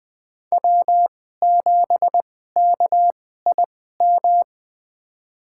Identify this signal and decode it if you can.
Morse code, heard as a keyed tone, about 20 wpm, spaced as for 14 wpm, 700 Hz. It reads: W7KIM
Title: USA Call Signs - 5 Characters - 2x Character Spacing - 20wpm